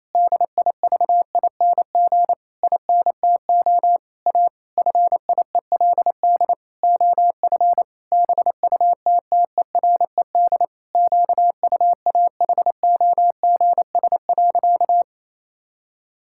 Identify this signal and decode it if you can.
Morse code, heard as a keyed tone, about 28 wpm, 700 Hz. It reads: DIVING INTO A FIELD OF BUTTERED QUAHOGS.
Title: Pangram Challenge v2 - 28wpm